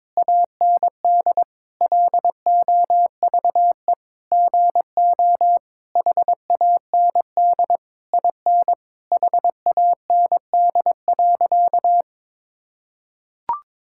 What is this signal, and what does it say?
Morse code, heard as a keyed tone, about 21 wpm, 700 Hz. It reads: AND LOVE GO HAND IN HAND.
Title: Sentences from Top 1000 Words - Review Part 5 - 22wpm